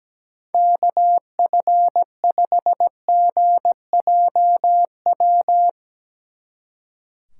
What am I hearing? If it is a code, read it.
Morse code, heard as a keyed tone, about 18 wpm, 700 Hz. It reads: KF5GJW
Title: Speed Racing - USA 2x3 Call Signs 17wpm